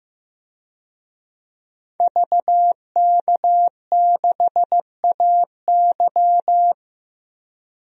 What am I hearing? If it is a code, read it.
Morse code, heard as a keyed tone, about 15 wpm, 700 Hz. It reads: VK6AY
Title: DXCC-100 Call Signs - 5 Characters - 15wpm